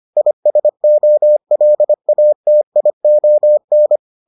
Morse code, heard as a keyed tone, about 25 wpm, 600 Hz. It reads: ISOLATION